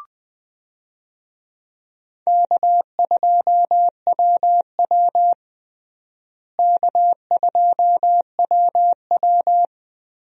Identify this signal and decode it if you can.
Morse code, heard as a keyed tone, about 20 wpm, 700 Hz. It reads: K2WW K2WW
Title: USA Call Signs - 4 Characters - Repeated 2x - 20wpm